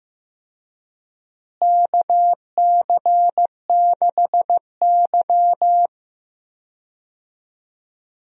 Morse code, heard as a keyed tone, about 15 wpm, 700 Hz. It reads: KC6Y
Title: USA Call Signs - 4 Characters - Repeated 3x - 15wpm